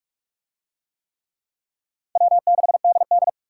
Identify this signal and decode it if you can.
Morse code, heard as a keyed tone, about 37 wpm, 700 Hz. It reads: W6DD
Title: Speed Racing - USA 1x2 Call Signs 30wpm